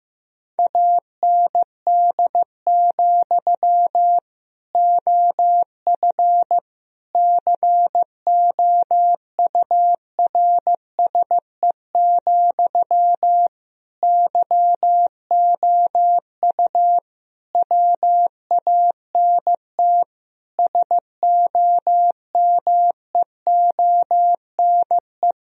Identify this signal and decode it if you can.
Morse code, heard as a keyed tone, about 15 wpm, 700 Hz. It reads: AND, OF COURSE, YOU WANT SOMEONE
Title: Sentences from Top 1000 Words - Review Part 1 - 15wpm